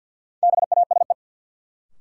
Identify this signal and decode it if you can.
Morse code, heard as a keyed tone, about 53 wpm, 700 Hz. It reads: BASE
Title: Sentences from Top 1600 Words - New Word Drill Alt - 50wpm